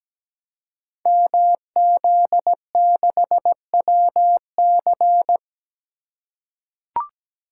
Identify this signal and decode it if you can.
Morse code, heard as a keyed tone, about 17 wpm, 700 Hz. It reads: MZ6WC
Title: Code Groups - 5 Letter:Numbers - 17wpm